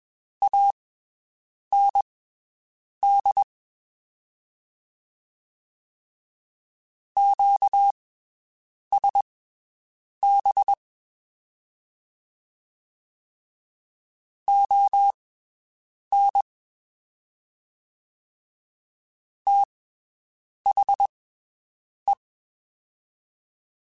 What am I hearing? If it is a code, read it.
Morse code, heard as a keyed tone, about 21 wpm, spaced as for 7 wpm, 800 Hz. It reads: AND QSB ON THE
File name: (spaced) / QSOs Spaced-mp3s / WD0BGP 1 @20 sp